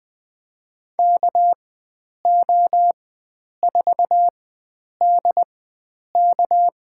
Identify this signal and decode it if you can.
Morse code, heard as a keyed tone, about 20 wpm, spaced as for 9 wpm, 700 Hz. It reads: KO4DK
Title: USA Call Signs - 5 Characters - 4x Character Spacing - 20wpm